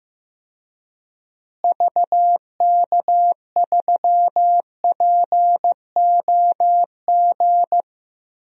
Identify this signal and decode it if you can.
Morse code, heard as a keyed tone, about 16 wpm, 700 Hz. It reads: VK3POG
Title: DXCC-100 Call Signs - 6 Characters - Repeated 2x - 15wpm